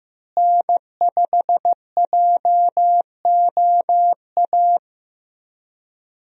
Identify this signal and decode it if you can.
Morse code, heard as a keyed tone, about 16 wpm, 700 Hz. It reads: N5JOA
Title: USA Call Signs - 5 Characters - Repeated 3x - 15wpm